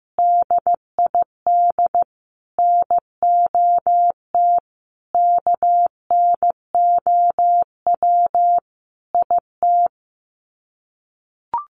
Morse code, heard as a keyed tone, about 15 wpm, 700 Hz. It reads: DID NOT KNOW IT
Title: Sets of 5 Words Encore2 15wpm